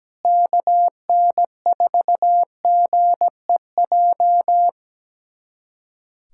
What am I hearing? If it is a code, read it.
Morse code, heard as a keyed tone, about 17 wpm, 700 Hz. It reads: KN4GEJ
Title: USA Callsign DE Callsign 17wpm